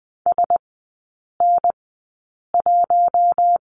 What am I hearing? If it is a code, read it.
Morse code, heard as a keyed tone, about 20 wpm, spaced as for 8 wpm, 700 Hz. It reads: SN1
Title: 047 Lesson 9e - Number 4 - Three Characters 20wpm